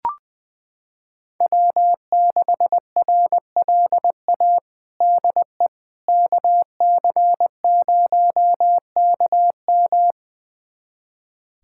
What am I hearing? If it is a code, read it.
Morse code, heard as a keyed tone, about 21 wpm, 700 Hz. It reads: W6RLA DE KC0KM
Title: USA Callsign DE Callsign - 5 Characters - 20wpm